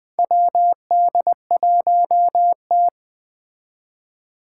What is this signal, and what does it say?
Morse code, heard as a keyed tone, about 20 wpm, 700 Hz. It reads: WD1T
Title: Four Challenging Letter-Number Combinations 20wpm